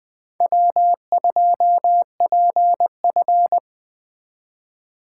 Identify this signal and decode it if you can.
Morse code, heard as a keyed tone, about 20 wpm, 700 Hz. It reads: W2PF
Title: Speed Racing - USA Call Signs 17wpm